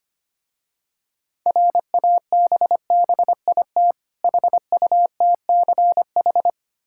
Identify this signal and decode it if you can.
Morse code, heard as a keyed tone, about 25 wpm, 700 Hz. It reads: RABBIT HUTCH
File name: practice / 25wpm / Difficult Words with Hints - 25wpm